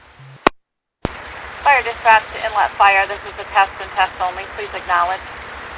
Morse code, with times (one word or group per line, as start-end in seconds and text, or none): none